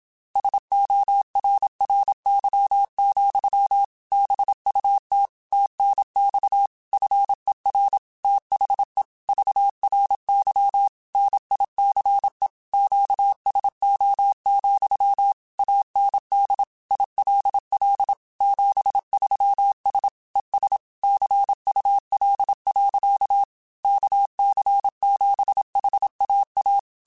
Morse code, heard as a keyed tone, about 27 wpm, 800 Hz. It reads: SORRY, BUT TNX FER THE VRY NICE QSO, AND ILL 73S ES CUL. KC7HAA